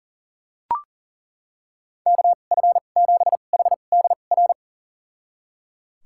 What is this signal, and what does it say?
Morse code, heard as a keyed tone, about 37 wpm, spaced as for 28 wpm, 700 Hz. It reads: KF7HDR
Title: USA Call Signs - 2x Character Spacing - 40wpm